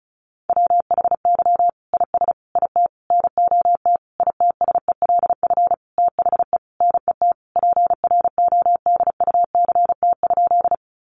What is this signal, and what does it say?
Morse code, heard as a keyed tone, about 34 wpm, 700 Hz. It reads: WHY IS IT NOT ITSELF THE NET PRODUCT?
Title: Sentences from Top 1400 Words 35wpm